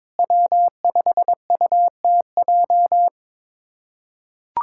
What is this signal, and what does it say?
Morse code, heard as a keyed tone, about 22 wpm, 700 Hz. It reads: W5UTJ
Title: USA Callsign DE Callsign 22wpm